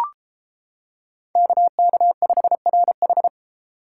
Morse code, heard as a keyed tone, about 31 wpm, 700 Hz. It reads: KK5RH